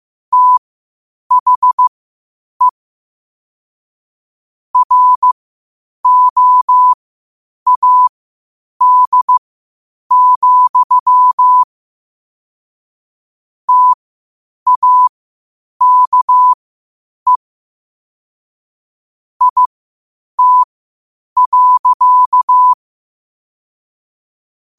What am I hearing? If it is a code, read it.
Morse code, heard as a keyed tone, about 15 wpm, spaced as for 8 wpm, 1000 Hz. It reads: THE ROAD, TAKE IT.
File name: qotdfiles / 8wpm